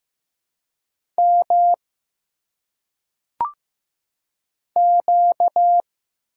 Morse code, heard as a keyed tone, about 15 wpm, 700 Hz. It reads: M Q
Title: Single Letters - Phonetic Pronunciation 15wpm